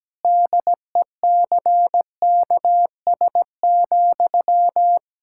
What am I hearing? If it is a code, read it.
Morse code, heard as a keyed tone, about 17 wpm, 700 Hz. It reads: DECKS,
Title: Pangram Challenge v2 - 17wpm